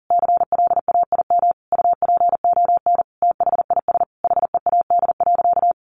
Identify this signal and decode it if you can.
Morse code, heard as a keyed tone, about 40 wpm, 700 Hz. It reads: CLAIM UPON THIS HEAD.
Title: Sentences from Top 1500 Words 40wpm